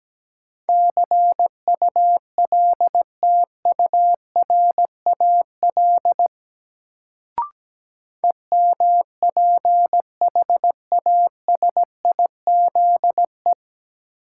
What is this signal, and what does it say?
Morse code, heard as a keyed tone, about 17 wpm, 700 Hz. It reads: CULTURAL EMPHASIZE
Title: Sentences from Top 1500 Words - New Word Drill Alt - 17wpm